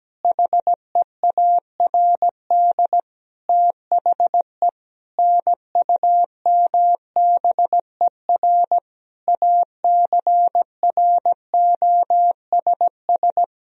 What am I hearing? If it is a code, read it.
Morse code, heard as a keyed tone, about 17 wpm, 700 Hz. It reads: HEARD THE NUMBER ACROSS
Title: Sentences from Top 1000 Words - Review Part 1 - 17wpm